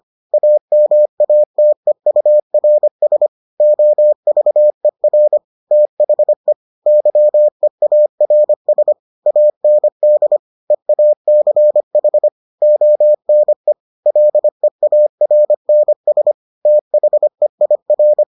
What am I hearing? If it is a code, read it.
Morse code, heard as a keyed tone, about 25 wpm, 600 Hz. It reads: AMATEURS OVER THE YEARS AND EACH ONE LEARNS THEIR